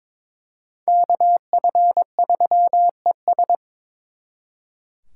Morse code, heard as a keyed tone, about 26 wpm, 700 Hz. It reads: KF3ES
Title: Speed Racing - USA 2x2 Call Signs 22wpm